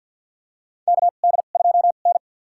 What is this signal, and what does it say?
Morse code, heard as a keyed tone, about 52 wpm, spaced as for 36 wpm, 700 Hz. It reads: KD2N
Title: USA Call Signs - 4 Characters - 2x Character Spacing - 50wpm